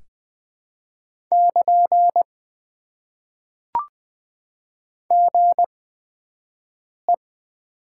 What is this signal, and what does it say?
Morse code, heard as a keyed tone, about 20 wpm, spaced as for 5 wpm, 700 Hz. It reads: <KN> GE
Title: Top 90 QSO Elements - 8x Character Spacing - 20wpm